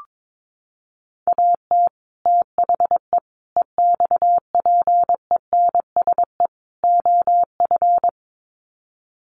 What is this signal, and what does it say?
Morse code, heard as a keyed tone, about 22 wpm, 700 Hz. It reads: AT THE EXPENSE OF
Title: Sets of 4 Words 22wpm